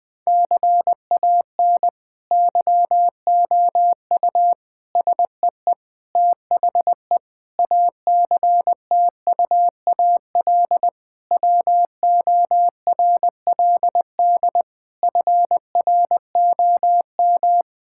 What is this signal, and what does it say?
Morse code, heard as a keyed tone, about 21 wpm, 700 Hz. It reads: CAN YOU SEE THE ACTUAL WORLD FROM
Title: Sentences from Top 1000 Words - Review Part 5 - 20wpm